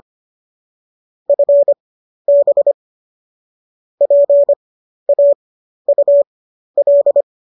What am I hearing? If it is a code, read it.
Morse code, heard as a keyed tone, about 25 wpm, spaced as for 12 wpm, 570 Hz. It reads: FB PAUL